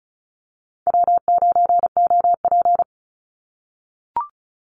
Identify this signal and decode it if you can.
Morse code, heard as a keyed tone, about 36 wpm, 700 Hz. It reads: W9OP
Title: Call Signs - 35wpm